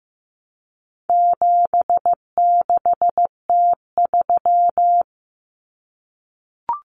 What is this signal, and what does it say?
Morse code, heard as a keyed tone, about 15 wpm, 700 Hz. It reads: 76T3
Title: Common Cut Numbers - 4-Digit 15wpm